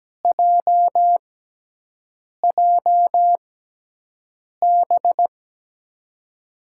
Morse code, heard as a keyed tone, about 17 wpm, spaced as for 6 wpm, 700 Hz. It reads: JJB
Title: USA Call Signs - 6 Characters - 6x Character Spacing - 17wpm